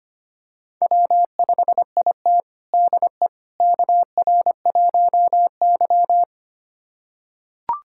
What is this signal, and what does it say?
Morse code, heard as a keyed tone, about 25 wpm, 700 Hz. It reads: W5IT DE KR1Y
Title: USA Callsign DE Callsign - 4 Characters - 25wpm